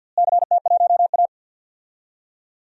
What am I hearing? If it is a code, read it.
Morse code, heard as a keyed tone, about 50 wpm, 700 Hz. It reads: CT1A
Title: DXCC-100 Call Signs - 4 Characters - Repeated 3x - 50wpm